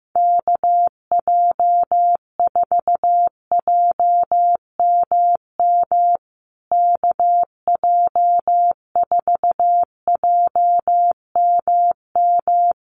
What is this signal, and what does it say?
Morse code, heard as a keyed tone, about 15 wpm, 700 Hz. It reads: KJ4JMM KJ4JMM
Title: USA Call Signs Repeated 3x 15wpm